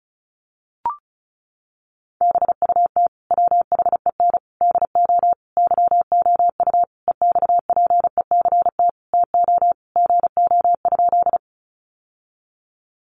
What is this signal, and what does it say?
Morse code, heard as a keyed tone, about 34 wpm, 700 Hz. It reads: BUT WHEN DO YOU EXPECT TO GO?
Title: Sentences from Top 1100 Words 35wpm